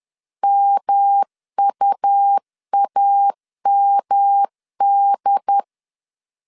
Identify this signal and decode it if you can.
Morse code, heard as a keyed tone, about 10 wpm, 800 Hz. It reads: MUAMD
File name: LEZIONI / 6 PKX0 / 6 P 10wpm 2432